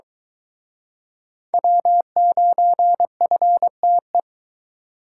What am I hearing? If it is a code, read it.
Morse code, heard as a keyed tone, about 22 wpm, 700 Hz. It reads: W9FTE